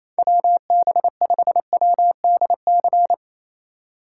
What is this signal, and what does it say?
Morse code, heard as a keyed tone, about 28 wpm, 700 Hz. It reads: WB5WDC
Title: USA Call Signs - 6 Characters - 4x Character Spacing - 28wpm